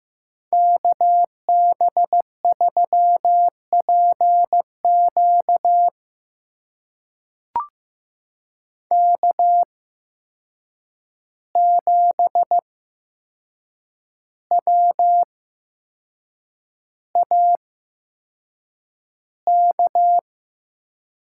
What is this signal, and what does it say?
Morse code, heard as a keyed tone, about 15 wpm, spaced as for 6 wpm, 700 Hz. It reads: KB3PQ K7WAK
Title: USA Call Signs - 5 Characters - 8x Character Spacing - 15wpm